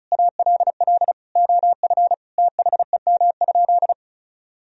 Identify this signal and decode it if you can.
Morse code, heard as a keyed tone, about 36 wpm, 700 Hz. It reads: ALL OF THEM?
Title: Sentences from Top 1000 Words - Review Part 4 - 35wpm